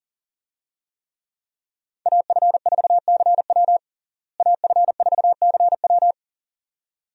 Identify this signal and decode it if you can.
Morse code, heard as a keyed tone, about 37 wpm, 700 Hz. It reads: AF4CW AF4CW
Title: USA Call Signs - 5 Characters - Repeated 2x - 40wpm